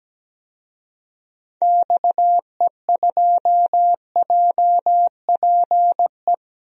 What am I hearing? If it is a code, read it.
Morse code, heard as a keyed tone, about 17 wpm, 700 Hz. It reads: XE2JPE
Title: DXCC-100 Call Signs - 6 Characters - Repeated 2x - 17wpm